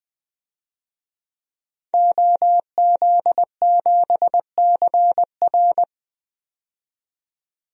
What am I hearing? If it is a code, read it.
Morse code, heard as a keyed tone, about 20 wpm, 700 Hz. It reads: OZ7CR
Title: DXCC-100 Call Signs - 5 Characters - 20wpm